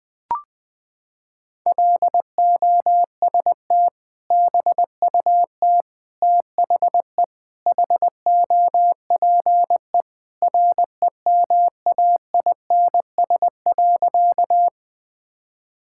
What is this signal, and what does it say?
Morse code, heard as a keyed tone, about 20 wpm, 700 Hz. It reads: LOST BUT THE HOPE REMAINS.
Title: Sentences with 2-Word Phrases - 20wpm